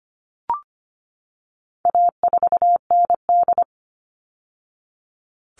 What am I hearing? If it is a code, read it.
Morse code, heard as a keyed tone, about 25 wpm, 700 Hz. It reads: A4ND
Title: All Cut Numbers - 4-Digit 25wpm